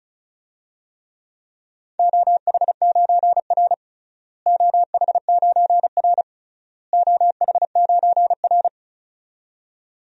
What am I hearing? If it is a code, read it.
Morse code, heard as a keyed tone, about 36 wpm, 700 Hz. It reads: OH9R OH9R OH9R
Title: DXCC-100 Call Signs - 4 Characters - Repeated 3x - 35wpm